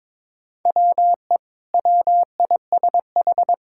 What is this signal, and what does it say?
Morse code, heard as a keyed tone, about 22 wpm, 700 Hz. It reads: WE WISH
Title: Sentences from Top 1600 Words 22wpm